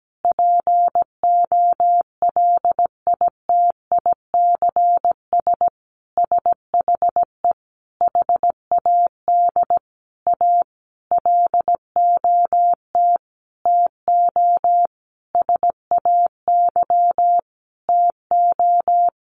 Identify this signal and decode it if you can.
Morse code, heard as a keyed tone, about 17 wpm, 700 Hz. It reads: POLITICS SHE HAD A LOT TO SAY TO
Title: Sentences from Top 1000 Words 17wpm